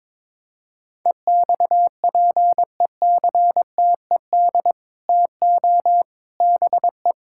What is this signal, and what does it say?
Morse code, heard as a keyed tone, about 21 wpm, 700 Hz. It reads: EXPECTED TO BE
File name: practice / 22wpm / Sets of 8 Words - 22wpm